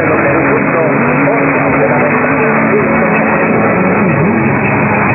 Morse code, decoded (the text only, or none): none